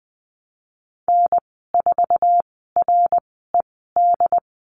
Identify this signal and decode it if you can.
Morse code, heard as a keyed tone, about 20 wpm, spaced as for 14 wpm, 700 Hz. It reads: N4RED
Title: 154 Lesson 25g - Letter p - Call Signs 20wpm